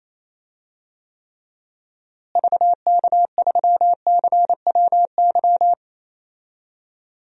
Morse code, heard as a keyed tone, about 28 wpm, 700 Hz. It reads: VK3CWY
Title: DXCC-100 Call Signs - 6 Characters - Repeated 2x - 28wpm